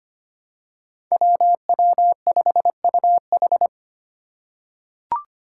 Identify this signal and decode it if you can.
Morse code, heard as a keyed tone, about 25 wpm, 700 Hz. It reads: WW5UH
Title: Five Challenging Letter-Number Combinations 25wpm